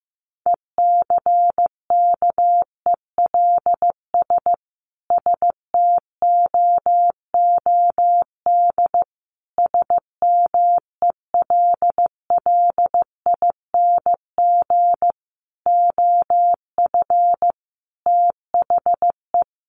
Morse code, heard as a keyed tone, about 15 wpm, 700 Hz. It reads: ECKELS STOOD SMELLING OF THE
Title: A Sound Of Thunder 15wpm